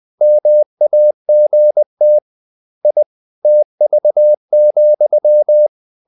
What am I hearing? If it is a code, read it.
Morse code, heard as a keyed tone, about 20 wpm, 600 Hz. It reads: MAGT I TV,
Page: Ferie 20 wpm